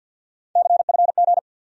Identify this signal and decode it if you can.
Morse code, heard as a keyed tone, about 50 wpm, 700 Hz. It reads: CFG
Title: USA Call Sign Postfixes - 3 Letters 50wpm